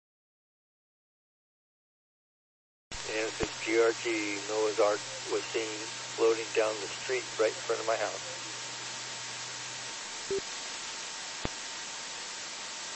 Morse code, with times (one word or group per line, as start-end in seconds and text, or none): none